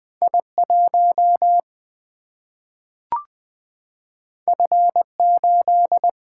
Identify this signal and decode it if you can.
Morse code, heard as a keyed tone, about 20 wpm, 700 Hz. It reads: I 1 F 8
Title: Single-Letter Single-Number 20wpm